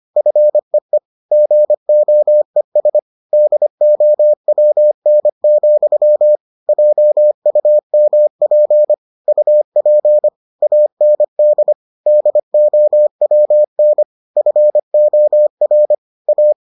Morse code, heard as a keyed tone, about 25 wpm, 600 Hz. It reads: FEE GOES DOWN, JUMP UP AND DOWN FOR A